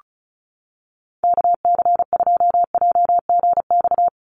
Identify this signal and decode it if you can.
Morse code, heard as a keyed tone, about 36 wpm, 700 Hz. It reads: KC2JGX
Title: USA Call Signs 35wpm